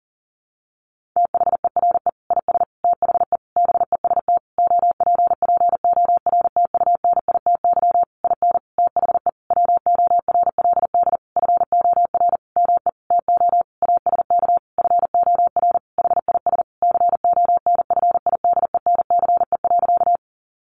Morse code, heard as a keyed tone, about 40 wpm, 700 Hz. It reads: THERE IS THE BEST OPPORTUNITY IN THE WORLD FOR ME TO ASK FOR HIS CONFIDENCE.
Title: Sentences from Top 1500 Words 40wpm